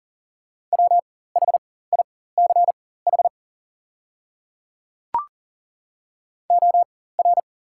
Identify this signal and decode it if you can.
Morse code, heard as a keyed tone, about 38 wpm, spaced as for 18 wpm, 690 Hz. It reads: WHICH OR
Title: 4x Character Spacing - Top 100 Words - 40wpm